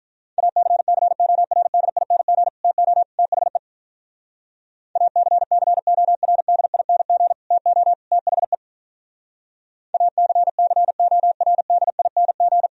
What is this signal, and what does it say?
Morse code, heard as a keyed tone, about 46 wpm, 700 Hz. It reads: ACCORDING TO THE ACCORDING TO THE ACCORDING